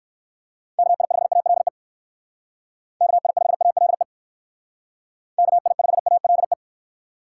Morse code, heard as a keyed tone, about 58 wpm, 700 Hz. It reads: KI5ALE KI5ALE KI5ALE